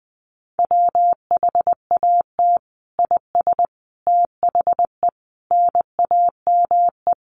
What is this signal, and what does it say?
Morse code, heard as a keyed tone, about 19 wpm, 700 Hz. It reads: WHAT IS THE NAME
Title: Sentences from Top 1500 Words 20wpm